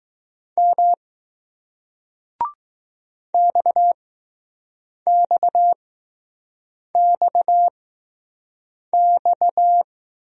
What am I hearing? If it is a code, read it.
Morse code, heard as a keyed tone, about 20 wpm, spaced as for 6 wpm, 700 Hz. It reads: M XXXX